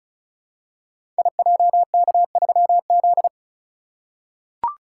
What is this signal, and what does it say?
Morse code, heard as a keyed tone, about 36 wpm, 700 Hz. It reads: IJK3Z
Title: Code Groups - 5 Letter:Numbers - 35wpm